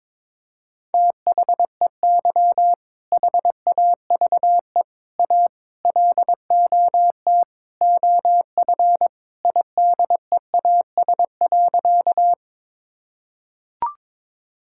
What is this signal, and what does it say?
Morse code, heard as a keyed tone, about 22 wpm, 700 Hz. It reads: THEY HAVE A LOT OF IDEAS.
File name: practice / 22wpm / Sentences with 4-Word Phrases - 22wpm